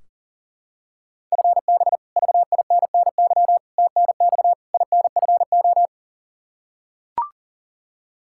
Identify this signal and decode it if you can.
Morse code, heard as a keyed tone, about 40 wpm, 700 Hz. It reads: FB VINNY TNX INFO
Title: Practice QSO - Exchange 3 - More Info - 40wpm